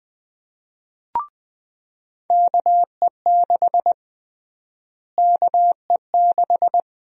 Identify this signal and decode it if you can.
Morse code, heard as a keyed tone, about 20 wpm, 700 Hz. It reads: KE6 KE6
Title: USA Call Sign Prefixes - Repeated 2x - 20wpm